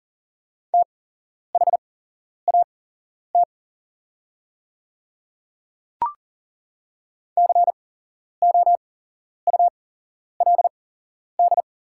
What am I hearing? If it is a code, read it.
Morse code, heard as a keyed tone, about 40 wpm, spaced as for 11 wpm, 700 Hz. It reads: THAT COULD
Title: 8x Character Spacing - Top 200 Words - 40wpm